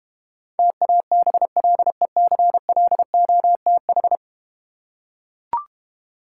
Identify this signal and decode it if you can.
Morse code, heard as a keyed tone, about 31 wpm, 700 Hz. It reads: TABLECLOTH